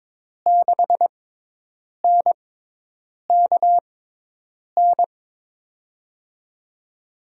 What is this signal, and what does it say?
Morse code, heard as a keyed tone, about 22 wpm, spaced as for 8 wpm, 700 Hz. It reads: 6NKN
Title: USA Call Signs - 6 Characters - 6x Character Spacing - 22wpm